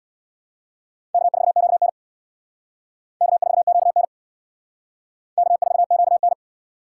Morse code, heard as a keyed tone, about 64 wpm, 700 Hz. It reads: K4QN K4QN K4QN